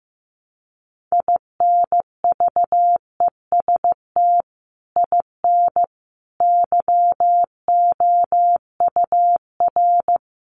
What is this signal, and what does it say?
Morse code, heard as a keyed tone, about 15 wpm, 700 Hz. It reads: INVEST IN YOUR
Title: Sentences from Top 1100 Words 15wpm